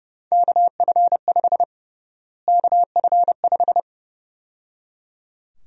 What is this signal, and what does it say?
Morse code, heard as a keyed tone, about 31 wpm, 700 Hz. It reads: KF5 KF5
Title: USA Call Sign Prefixes - Repeated 2x - 30wpm